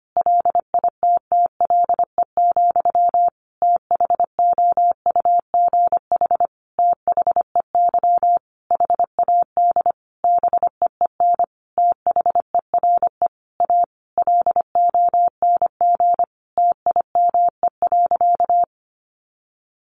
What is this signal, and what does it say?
Morse code, heard as a keyed tone, about 25 wpm, 700 Hz. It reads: LITTLE, THOUGH THEY HAD BEEN THERE A LONG TIME.
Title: Sentences from Top 400 Words 25wpm